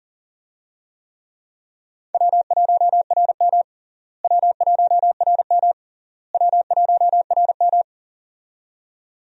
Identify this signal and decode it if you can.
Morse code, heard as a keyed tone, about 40 wpm, 700 Hz. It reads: W1RM W1RM W1RM
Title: USA Call Signs - 4 Characters - Repeated 3x - 40wpm